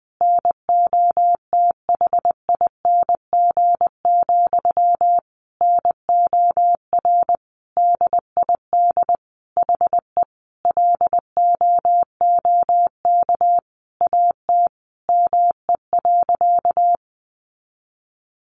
Morse code, heard as a keyed tone, about 20 wpm, 700 Hz. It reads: NOTHING, NOR DID HE LOOK AT ME.
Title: Sentences from Top 1500 Words 20wpm